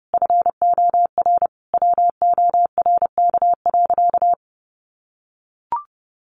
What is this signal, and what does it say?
Morse code, heard as a keyed tone, about 30 wpm, 700 Hz. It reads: FOR WORK.
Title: Sentences from Top 1400 Words 30wpm